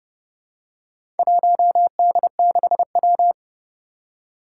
Morse code, heard as a keyed tone, about 30 wpm, 700 Hz. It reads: 1D6W
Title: Four Challenging Letter-Number Combinations 30wpm